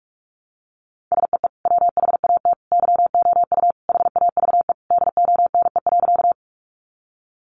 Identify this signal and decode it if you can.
Morse code, heard as a keyed tone, about 44 wpm, 700 Hz. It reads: SEE WHAT YOU HAVE DONE.
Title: Sentences from Top 500 Words 45wpm